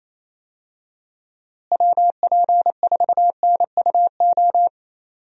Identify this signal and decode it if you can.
Morse code, heard as a keyed tone, about 27 wpm, 700 Hz. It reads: WP4NUO
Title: USA Call Signs - 6 Characters - Repeated 3x - 28wpm